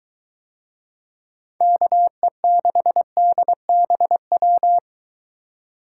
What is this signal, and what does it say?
Morse code, heard as a keyed tone, about 23 wpm, 700 Hz. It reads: KE6DBW